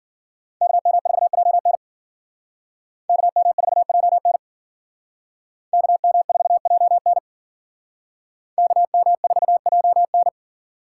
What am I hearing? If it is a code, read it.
Morse code, heard as a keyed tone, about 48 wpm, 700 Hz. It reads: KM4JN KM4JN KM4JN KM4JN
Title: Speed Racing - USA 2x2 Call Signs 40wpm